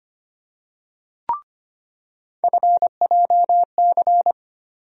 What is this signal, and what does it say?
Morse code, heard as a keyed tone, about 25 wpm, 700 Hz. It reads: FJC
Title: USA Call Sign Postfixes - 3 Letters 25wpm